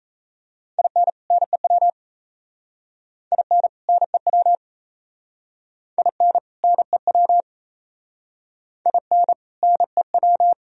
Morse code, heard as a keyed tone, about 36 wpm, 700 Hz. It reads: IN NEW IN NEW IN NEW IN NEW